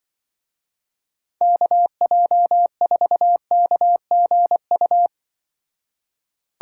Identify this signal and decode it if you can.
Morse code, heard as a keyed tone, about 25 wpm, 700 Hz. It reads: KJ4KGU
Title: Speed Racing - USA Call Signs 20wpm